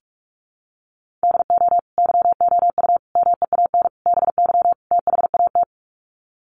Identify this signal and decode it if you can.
Morse code, heard as a keyed tone, about 44 wpm, 700 Hz. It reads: DO YOU MEAN BY THAT
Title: Sets of 5 Words Encore2 45wpm